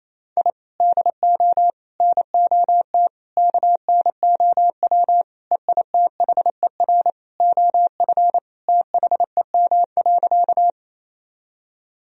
Morse code, heard as a keyed tone, about 28 wpm, 700 Hz. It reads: I DO NOT KNOW EITHER OF THEM.
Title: Sentences from Top 1000 Words - Review Part 4 - 28wpm